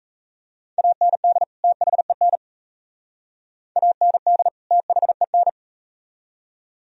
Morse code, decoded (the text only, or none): AND THEN AND THEN